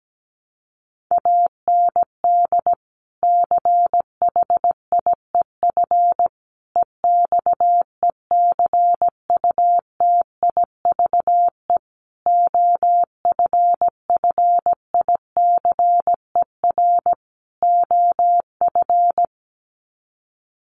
Morse code, heard as a keyed tone, about 17 wpm, 700 Hz. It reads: AND CHIEF EXECUTIVE OFFICER OF
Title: Sets of 5 Words Encore2 17wpm